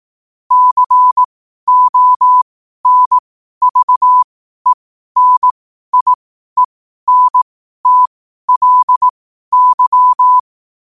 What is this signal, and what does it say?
Morse code, heard as a keyed tone, about 18 wpm, spaced as for 12 wpm, 1000 Hz. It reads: CONVENIENTLY
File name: qotdfiles / 12wpm